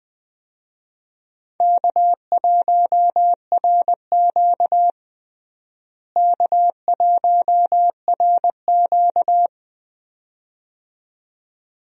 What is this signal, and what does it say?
Morse code, heard as a keyed tone, about 20 wpm, 700 Hz. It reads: K1RQ K1RQ
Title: USA Call Signs - 4 Characters - Repeated 2x - 20wpm